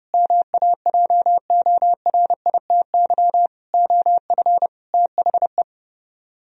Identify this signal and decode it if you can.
Morse code, heard as a keyed tone, about 38 wpm, 700 Hz. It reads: MAJORITY OF THE